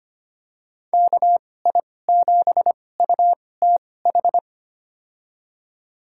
Morse code, heard as a keyed tone, about 25 wpm, spaced as for 18 wpm, 700 Hz. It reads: KI7UTH
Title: USA Call Signs - 2x Character Spacing - 25wpm